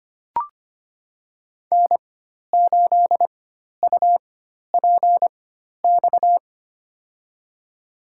Morse code, heard as a keyed tone, about 25 wpm, spaced as for 12 wpm, 700 Hz. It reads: N8UPX
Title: USA Call Signs - 5 Characters - 4x Character Spacing - 25wpm